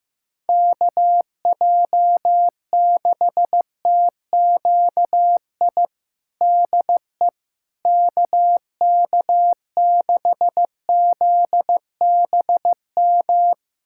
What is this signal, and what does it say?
Morse code, heard as a keyed tone, about 15 wpm, 700 Hz. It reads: KJ6TQI DE KK6ZBM
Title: USA Callsign DE Callsign 15wpm